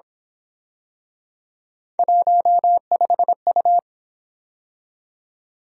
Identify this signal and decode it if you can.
Morse code, heard as a keyed tone, about 26 wpm, 700 Hz. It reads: 15U
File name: Speed-Racing - Three Letter-Number Combinations - 22wpm